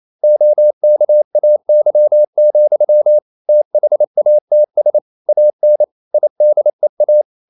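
Morse code, heard as a keyed tone, about 28 wpm, 600 Hz. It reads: OKAY, THATS AN IDEA
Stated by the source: Morse code practice at 28 wpm, text of Ken Follett's novel Never in English, from CW med Gnister.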